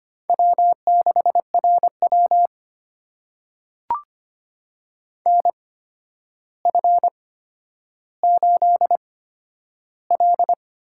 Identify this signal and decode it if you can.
Morse code, heard as a keyed tone, about 25 wpm, spaced as for 11 wpm, 700 Hz. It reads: W6RW NF8L
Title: USA Call Signs - 8x Character Spacing - 25wpm